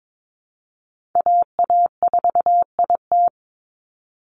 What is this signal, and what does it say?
Morse code, heard as a keyed tone, about 22 wpm, 700 Hz. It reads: AA4IT
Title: NA SOTA Chaser Callsigns 22wpm